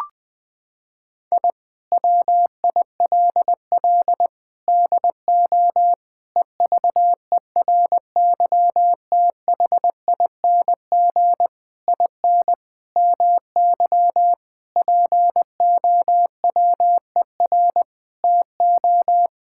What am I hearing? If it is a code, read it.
Morse code, heard as a keyed tone, about 20 wpm, 700 Hz. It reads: I WILL DO EVERYTHING IN MY POWER TO
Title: Sentences with 8-Word Phrases - 20wpm